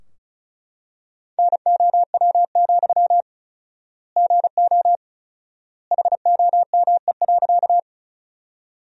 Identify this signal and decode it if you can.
Morse code, heard as a keyed tone, about 36 wpm, 700 Hz. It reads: NOW, GO HOME.
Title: 4x Word-Spacing - Sentences from Top 200 Words - 35wpm